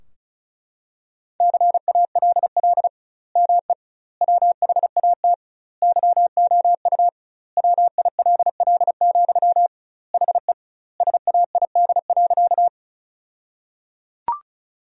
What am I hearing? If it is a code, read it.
Morse code, heard as a keyed tone, about 36 wpm, 700 Hz. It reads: CALL ME WHAT YOU WILL, HE SAID.
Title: 2x Word-Spacing - Sentences from Top 200 Words - 35wpm